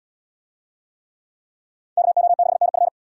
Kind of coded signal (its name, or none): Morse code